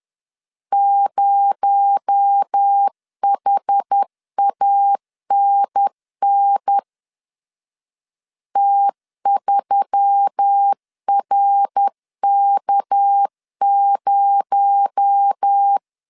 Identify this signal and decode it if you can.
Morse code, heard as a keyed tone, about 10 wpm, 800 Hz. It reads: 0HANN T3RK0